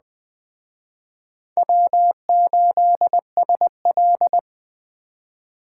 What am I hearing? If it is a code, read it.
Morse code, heard as a keyed tone, about 21 wpm, 700 Hz. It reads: W8SL
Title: Speed Racing - USA 1x2 Call Signs 17wpm